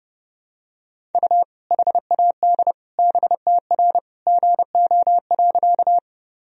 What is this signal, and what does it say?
Morse code, heard as a keyed tone, about 32 wpm, 700 Hz. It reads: U HAD BTR GO.
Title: Sentences from Top 400 Words and Abbreviations 30wpm